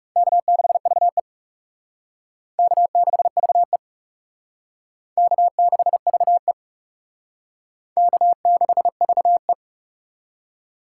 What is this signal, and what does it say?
Morse code, heard as a keyed tone, about 36 wpm, 700 Hz. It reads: K6VE K6VE K6VE K6VE
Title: Speed Racing - USA Call Signs 30wpm